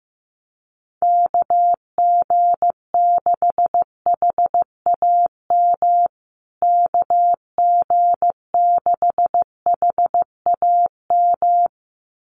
Morse code, heard as a keyed tone, about 15 wpm, 700 Hz. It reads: KG6HAM KG6HAM
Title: USA Call Signs Repeated 2x 15wpm